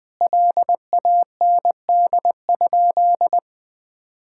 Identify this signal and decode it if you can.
Morse code, heard as a keyed tone, about 20 wpm, 700 Hz. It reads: LAND?
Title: Sentences from Top 1600 Words 20wpm